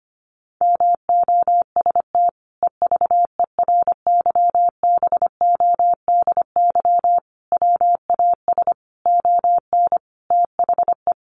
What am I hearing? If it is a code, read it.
Morse code, heard as a keyed tone, about 25 wpm, 700 Hz. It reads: MOST EVERYBODY WAS ON THE
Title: Sentences from Top 1400 Words 25wpm